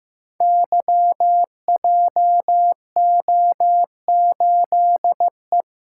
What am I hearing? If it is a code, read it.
Morse code, heard as a keyed tone, about 15 wpm, 700 Hz. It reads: YJO8E
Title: Code Groups - 5 Letter:Numbers - 15wpm